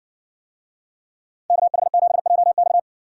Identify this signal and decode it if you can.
Morse code, heard as a keyed tone, about 60 wpm, 700 Hz. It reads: KH7JX